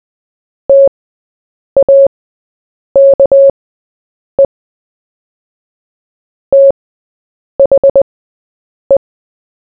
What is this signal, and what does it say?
Morse code, heard as a keyed tone, about 20 wpm, spaced as for 8 wpm, 550 Hz. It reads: TAKE THE
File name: Short-Story-4-8wpm